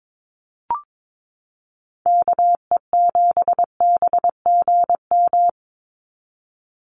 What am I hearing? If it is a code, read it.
Morse code, heard as a keyed tone, about 22 wpm, 700 Hz. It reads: KE7BGM
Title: NA SOTA Activator Callsigns 22wpm